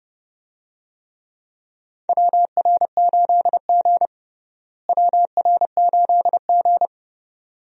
Morse code, heard as a keyed tone, about 32 wpm, 700 Hz. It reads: WR8G WR8G
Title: USA Call Signs - 4 Characters - Repeated 2x - 30wpm